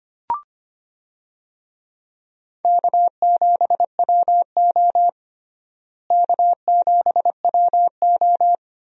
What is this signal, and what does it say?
Morse code, heard as a keyed tone, about 25 wpm, 700 Hz. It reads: K7WO K7WO
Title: USA Call Signs - 4 Characters - Repeated 2x - 25wpm